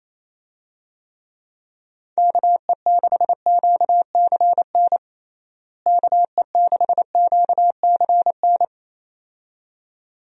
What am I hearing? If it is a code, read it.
Morse code, heard as a keyed tone, about 28 wpm, 700 Hz. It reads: KE6QCN KE6QCN
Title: USA Call Signs - 6 Characters - Repeated 2x - 28wpm